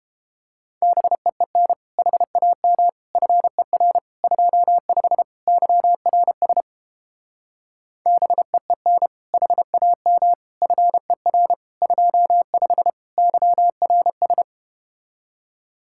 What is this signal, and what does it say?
Morse code, heard as a keyed tone, about 31 wpm, 700 Hz. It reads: BEEN HAM FER 25 YRS BEEN HAM FER 25 YRS